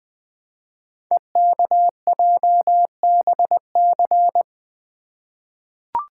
Morse code, heard as a keyed tone, about 20 wpm, 700 Hz. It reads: EKJBC
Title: Code Groups - 5 Letters - 20wpm